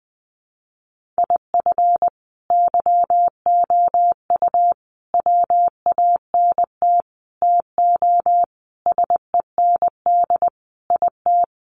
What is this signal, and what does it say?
Morse code, heard as a keyed tone, about 20 wpm, 700 Hz. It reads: IF YOU WANT TO SEND IT
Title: Sentences from Top 1500 Words 20wpm